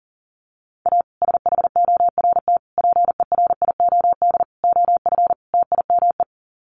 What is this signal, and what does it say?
Morse code, heard as a keyed tone, about 40 wpm, 700 Hz. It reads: A SHORT PERIOD OF TIME
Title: Sets of 5 Words Encore 40wpm